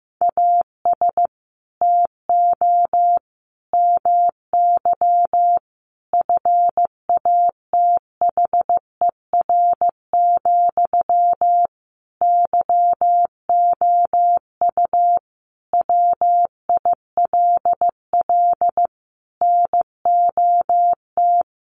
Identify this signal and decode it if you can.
Morse code, heard as a keyed tone, about 15 wpm, 700 Hz. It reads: AS TO MY FATHER, YOU WILL NOT
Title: Sentences from Top 600 Words 15wpm